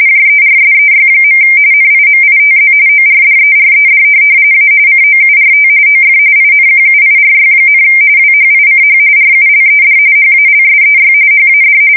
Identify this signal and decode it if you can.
RTTY, about 45.45 baud. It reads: JERSEY. NIMITZ ORDERED A COURT OF INQUIRY, WHICH RULED HALSEY HAD
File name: fsk170 / USS-New-Jersey